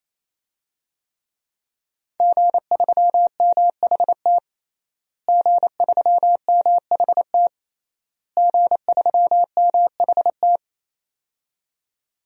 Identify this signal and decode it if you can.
Morse code, heard as a keyed tone, about 27 wpm, 700 Hz. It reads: G3MHT G3MHT G3MHT
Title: DXCC-100 Call Signs - 5 Characters - Repeated 3x - 28wpm